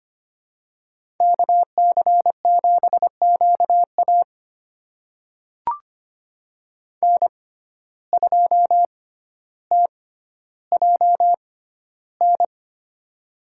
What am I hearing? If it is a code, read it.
Morse code, heard as a keyed tone, about 25 wpm, spaced as for 13 wpm, 700 Hz. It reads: KC7QA N2TJN
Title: USA Call Signs - 6x Character Spacing - 25wpm